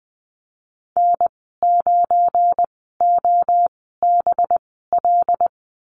Morse code, heard as a keyed tone, about 19 wpm, spaced as for 14 wpm, 700 Hz. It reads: N9OBL
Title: 193 Lesson 31g - Letter v - Call Signs 20wpm